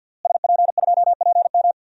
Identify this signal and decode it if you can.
Morse code, heard as a keyed tone, about 46 wpm, 700 Hz. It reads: SP2PM
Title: DXCC-100 Call Signs - 5 Characters - Repeated 3x - 50wpm